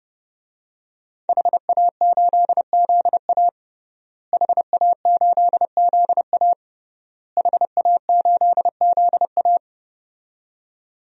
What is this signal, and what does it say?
Morse code, heard as a keyed tone, about 30 wpm, 700 Hz. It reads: HA8ZA HA8ZA HA8ZA
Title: DXCC-100 Call Signs - 5 Characters - Repeated 3x - 30wpm